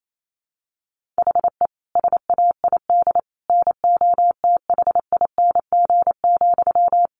Morse code, heard as a keyed tone, about 28 wpm, 700 Hz. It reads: HE SAID NOTHING,
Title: Sentences from Top 1500 Words 28wpm